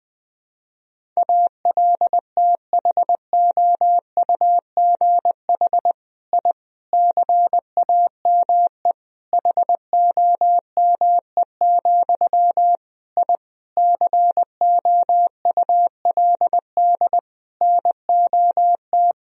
Morse code, heard as a keyed tone, about 20 wpm, 700 Hz. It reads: ALTHOUGH I CAME HOME, I COULD NOT
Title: Sentences from Top 1000 Words - Review Part 3 - 20wpm